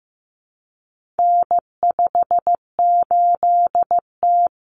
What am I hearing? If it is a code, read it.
Morse code, heard as a keyed tone, about 15 wpm, 700 Hz. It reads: N58T
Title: Common Cut Numbers - 4-Digit 15wpm